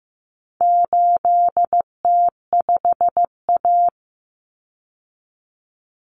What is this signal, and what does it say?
Morse code, heard as a keyed tone, about 15 wpm, 700 Hz. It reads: 8T5A
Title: Common Cut Numbers - 4-Digit 15wpm